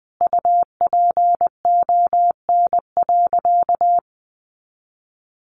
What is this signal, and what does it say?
Morse code, heard as a keyed tone, about 20 wpm, 700 Hz. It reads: UPON.
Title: Sentences from Top 1400 Words 20wpm